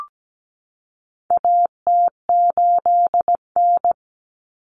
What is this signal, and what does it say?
Morse code, heard as a keyed tone, about 17 wpm, 700 Hz. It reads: AT8N
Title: Common Cut Numbers - 4-Digit 17wpm